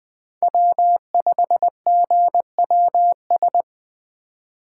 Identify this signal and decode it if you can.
Morse code, heard as a keyed tone, about 20 wpm, 700 Hz. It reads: W5GWS
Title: USA Call Signs - 5 Characters - 2x Character Spacing - 20wpm